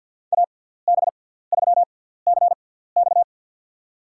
Morse code, heard as a keyed tone, about 48 wpm, spaced as for 17 wpm, 700 Hz. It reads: AB3CX
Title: USA Call Signs - 5 Characters - 6x Character Spacing - 50wpm